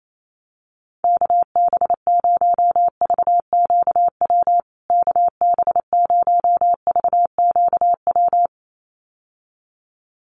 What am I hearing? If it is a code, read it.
Morse code, heard as a keyed tone, about 28 wpm, 700 Hz. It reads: KB0VQW KB0VQW
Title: USA Call Signs Repeated 2x 28wpm